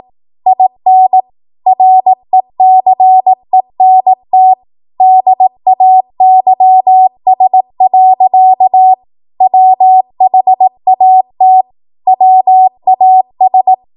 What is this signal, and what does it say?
Morse code, heard as a keyed tone, about 18 wpm, 750 Hz. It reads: IN RECENT DAYS. WHAT WAS
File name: Archive / 18 WPM / 18WPM